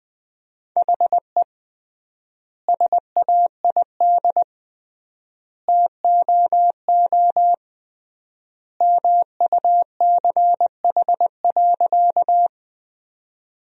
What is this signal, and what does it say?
Morse code, heard as a keyed tone, about 20 wpm, 700 Hz. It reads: HE SAID TOO MUCH.
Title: 3x Word-Spacing - Sentences from Top 200 Words - 20wpm